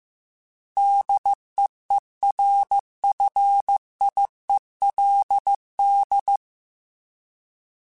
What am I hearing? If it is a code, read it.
Morse code, heard as a keyed tone, about 15 wpm, 800 Hz. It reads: DEERFIELD
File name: fromSMs / FromSMs mp3s / Cities@13